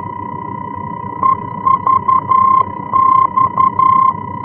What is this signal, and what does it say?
Morse code, heard as a keyed tone, about 11 wpm, 1050 Hz. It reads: EVX
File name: france / mx / EVX-F-VOR